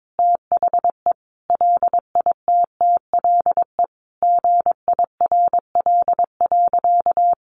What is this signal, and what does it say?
Morse code, heard as a keyed tone, about 22 wpm, 700 Hz. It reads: THE LITTLE GIRL.
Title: Sentences from Top 400 Words 22wpm